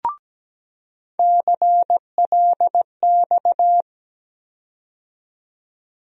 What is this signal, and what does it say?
Morse code, heard as a keyed tone, about 17 wpm, 700 Hz. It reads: CLX
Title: USA Call Sign Postfixes - 3 Letters 17wpm